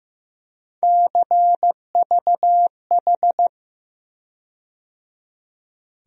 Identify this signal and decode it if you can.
Morse code, heard as a keyed tone, about 15 wpm, 700 Hz. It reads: CVH
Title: USA Call Sign Postfixes - 3 Letters 15wpm